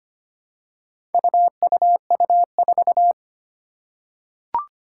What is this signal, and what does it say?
Morse code, heard as a keyed tone, about 25 wpm, 700 Hz. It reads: UUU4
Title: Four Challenging Letter-Number Combinations 25wpm